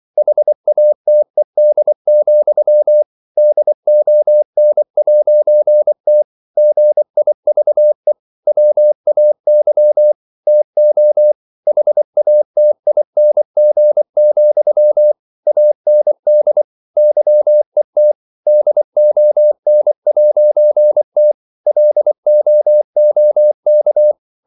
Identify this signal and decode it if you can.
Morse code, heard as a keyed tone, about 24 wpm, 600 Hz. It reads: HATED, DON'T GIVE WAY TO HATING, AND YET DON'T LOOK
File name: CW / Practice / If24